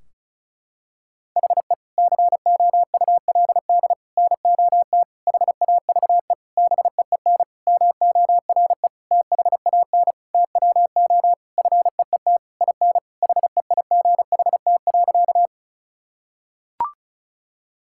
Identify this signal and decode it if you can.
Morse code, heard as a keyed tone, about 36 wpm, 700 Hz. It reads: HE COULD NOT HAVE BEEN MORE THAN TWO FEET IN HEIGHT.
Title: Sentences from Top 1600 Words 35wpm